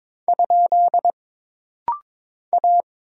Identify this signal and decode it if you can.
Morse code, heard as a keyed tone, about 22 wpm, 700 Hz. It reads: ? A
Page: This rapid-fire timing is geared to push your ICR (Instant Character Recognition) skills.